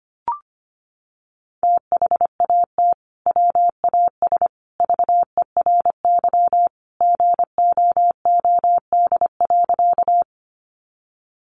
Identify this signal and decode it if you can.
Morse code, heard as a keyed tone, about 25 wpm, 700 Hz. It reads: THAT WAS VERY GOOD.
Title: Sentences from Top 200 Words 25wpm